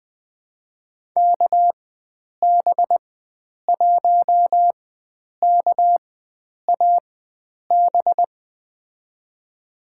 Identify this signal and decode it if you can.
Morse code, heard as a keyed tone, about 20 wpm, spaced as for 9 wpm, 700 Hz. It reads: KB1KAB
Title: USA Call Signs - 6 Characters - 4x Character Spacing - 20wpm